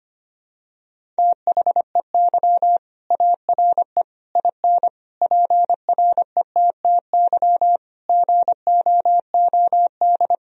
Morse code, heard as a keyed tone, about 25 wpm, 700 Hz. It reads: THEY ARE IN PRETTY GOOD
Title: Sentences from Top 1000 Words - Review Part 3 - 25wpm